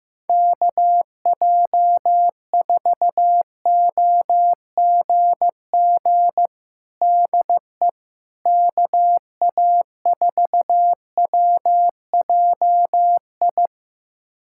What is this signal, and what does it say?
Morse code, heard as a keyed tone, about 15 wpm, 700 Hz. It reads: KJ4OGG DE KA4WJI
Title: USA Callsign DE Callsign - 6 Characters - 15wpm